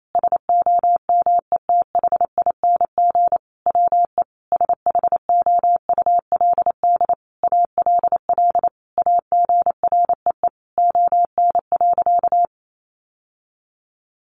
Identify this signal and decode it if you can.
Morse code, heard as a keyed tone, about 28 wpm, 700 Hz. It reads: SOMETHING WE SHOULD ALL AGREE ON.
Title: Sentences from Top 1100 Words 28wpm